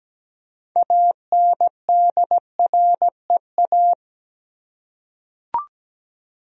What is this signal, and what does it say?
Morse code, heard as a keyed tone, about 17 wpm, 700 Hz. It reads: ANDREA
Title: USA Common Names - 17wpm